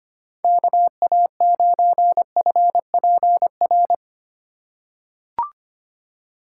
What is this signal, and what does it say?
Morse code, heard as a keyed tone, about 26 wpm, 700 Hz. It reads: KA9FPR
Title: USA Call Signs - 6 Characters - 8x Character Spacing - 25wpm